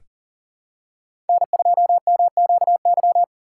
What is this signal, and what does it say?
Morse code, heard as a keyed tone, about 40 wpm, 700 Hz. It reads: N2MQY
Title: USA Call Signs - 5 Characters - 6x Character Spacing - 40wpm